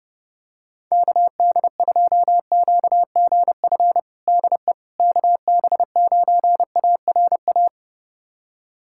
Morse code, heard as a keyed tone, about 32 wpm, 700 Hz. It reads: KD2QGF DE KB9ARA
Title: USA Callsign DE Callsign - 6 Characters - 30wpm